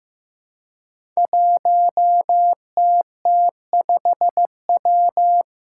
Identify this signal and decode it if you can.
Morse code, heard as a keyed tone, about 15 wpm, 700 Hz. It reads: 1TT5W
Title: Five Challenging Letter-Number Combinations 15wpm